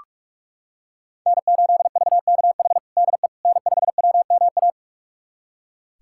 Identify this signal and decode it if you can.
Morse code, heard as a keyed tone, about 46 wpm, 700 Hz. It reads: N8VKH DE N5WMA
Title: USA Callsign DE Callsign - 5 Characters - 45wpm